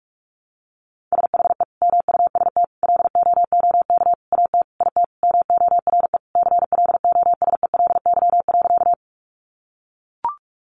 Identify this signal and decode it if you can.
Morse code, heard as a keyed tone, about 44 wpm, 700 Hz. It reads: SHE MUST LOOK AT IT MORE CLOSELY.
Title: Sentences from Top 1400 Words 45wpm